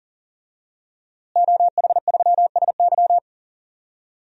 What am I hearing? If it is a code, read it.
Morse code, heard as a keyed tone, about 40 wpm, 700 Hz. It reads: OH3SY
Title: DXCC-100 Call Signs - 5 Characters - Repeated 2x - 40wpm